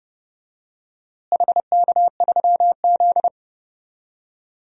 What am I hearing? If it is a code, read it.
Morse code, heard as a keyed tone, about 32 wpm, 700 Hz. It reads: HK3Z
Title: DXCC-100 Call Signs - 4 Characters - Repeated 3x - 30wpm